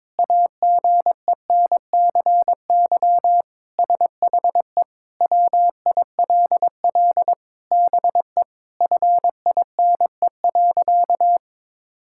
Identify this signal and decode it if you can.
Morse code, heard as a keyed tone, about 22 wpm, 700 Hz. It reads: AGENCY SHE WILL BE FINE.
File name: Sentences with 7-Word Phrases - 22wpm